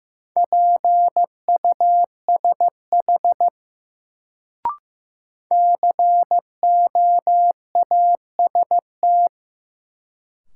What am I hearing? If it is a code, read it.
Morse code, heard as a keyed tone, about 15 wpm, 700 Hz. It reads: PUSH COAST
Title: Sentences from Top 1300 Words - New Word Drill Alt - 15wpm